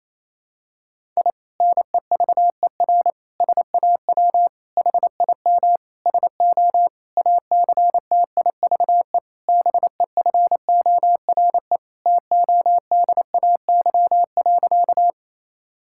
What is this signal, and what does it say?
Morse code, heard as a keyed tone, about 28 wpm, 700 Hz. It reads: I NEVER SAW HIM SO ACTIVE BEFORE TODAY.
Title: Sentences from Top 1000 Words - Review Part 6 - 28wpm